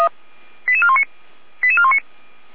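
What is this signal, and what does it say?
Five-tone selective call (ZVEI), heard as quick runs of five tones, tones of about 70 ms. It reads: 88419 88419
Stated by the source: ZVEI five-tone selective call 88235 received by SDR, from the feuerwehr-orlamuende (Orlamünde fire brigade) audio server.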